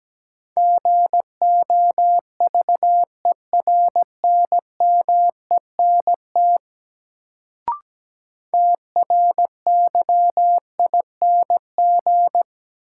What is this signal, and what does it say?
Morse code, heard as a keyed tone, about 17 wpm, 700 Hz. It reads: GOVERNMENT TRYING
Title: Sets of 2 Words - New Word Drill - 17wpm